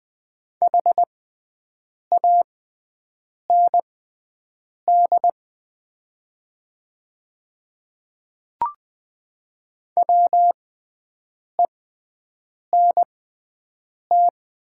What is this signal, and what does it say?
Morse code, heard as a keyed tone, about 20 wpm, spaced as for 7 wpm, 700 Hz. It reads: HAND WENT
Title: 6x Character Spacing - Top 200 Words - 20wpm